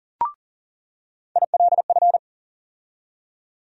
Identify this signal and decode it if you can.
Morse code, heard as a keyed tone, about 40 wpm, 700 Hz. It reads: ILF
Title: USA Call Sign Postfixes - 3 Letters 40wpm